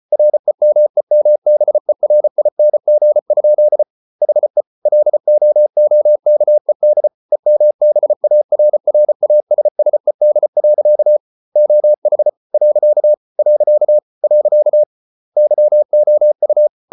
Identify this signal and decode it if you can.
Morse code, heard as a keyed tone, about 34 wpm, 600 Hz. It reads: REMEMBERING? HE LOOKED EMBARRASSED. OH . . YOU